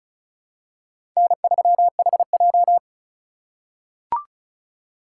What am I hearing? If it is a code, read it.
Morse code, heard as a keyed tone, about 34 wpm, 700 Hz. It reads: N3HJ
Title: USA Call Signs - 4 Characters - 3x Character Spacing - 35wpm